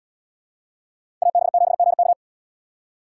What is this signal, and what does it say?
Morse code, heard as a keyed tone, about 67 wpm, 700 Hz. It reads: AB7GK